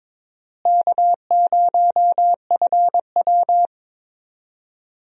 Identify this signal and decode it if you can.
Morse code, heard as a keyed tone, about 22 wpm, 700 Hz. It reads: K0FW
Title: USA Call Signs - 4 Characters - Repeated 2x - 22wpm